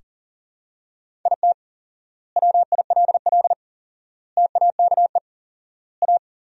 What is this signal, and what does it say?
Morse code, heard as a keyed tone, about 40 wpm, 700 Hz. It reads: IT WILL TAKE A
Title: 4x Word-Spacing - Sentences from Top 200 Words - 40wpm